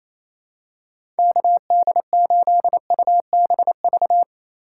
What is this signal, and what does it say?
Morse code, heard as a keyed tone, about 28 wpm, 700 Hz. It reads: KD8UBV
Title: USA Call Signs - 2x Character Spacing - 28wpm